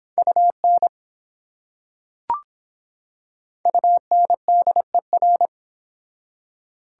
Morse code, heard as a keyed tone, about 25 wpm, 700 Hz. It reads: UN UNDER